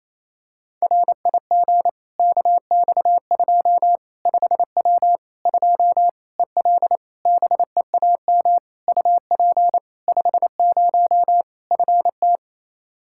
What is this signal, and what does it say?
Morse code, heard as a keyed tone, about 28 wpm, 700 Hz. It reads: RIG KX2 5W 2 EL BEAM UP 50 FT
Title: Practice QSO - Exchange 2 - Info - 28wpm